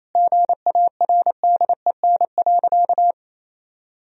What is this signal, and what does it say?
Morse code, heard as a keyed tone, about 28 wpm, 700 Hz. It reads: GARDEN.
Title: Sentences from Top 1000 Words - Review Part 4 - 28wpm